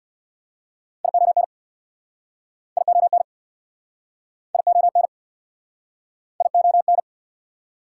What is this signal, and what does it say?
Morse code, heard as a keyed tone, about 69 wpm, 700 Hz. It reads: ION ION ION ION